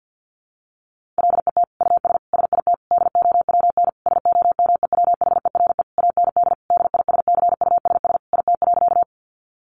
Morse code, heard as a keyed tone, about 50 wpm, 700 Hz. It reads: LET US SIT DOWN SOMEWHERE AND DISCUSS IT.
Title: Sentences from Top 1400 Words 50wpm